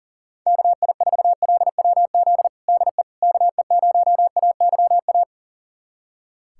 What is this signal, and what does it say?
Morse code, heard as a keyed tone, about 40 wpm, 700 Hz. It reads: KI4LWZ DE KE0AYA
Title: USA Callsign DE Callsign - 6 Characters - 40wpm